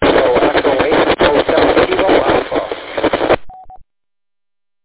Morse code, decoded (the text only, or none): none